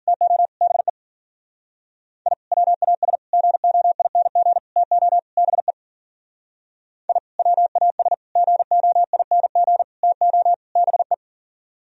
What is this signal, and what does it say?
Morse code, heard as a keyed tone, about 42 wpm, 700 Hz. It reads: TO BE I WAS GOING TO BE I WAS GOING TO BE
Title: Speed-Racing - Sets of 5 Words 40wpm